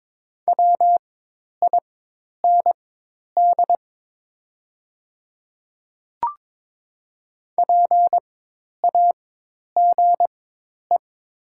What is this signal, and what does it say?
Morse code, heard as a keyed tone, about 22 wpm, spaced as for 10 wpm, 700 Hz. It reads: WIND PAGE
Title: 4x Character Spacing - Top 500 Words - 22wpm